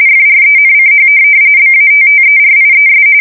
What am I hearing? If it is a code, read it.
RTTY, about 45.45 baud. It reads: FRENCH BATTERIES